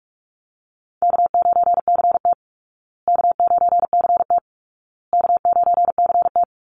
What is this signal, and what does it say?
Morse code, heard as a keyed tone, about 44 wpm, 700 Hz. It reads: K9CT K9CT K9CT
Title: Easy Call Signs Repeated 3x - 45wpm